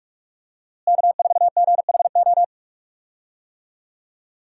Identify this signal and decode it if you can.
Morse code, heard as a keyed tone, about 45 wpm, 700 Hz. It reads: K4GHO
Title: USA Call Signs - 5 Characters - Repeated 2x - 45wpm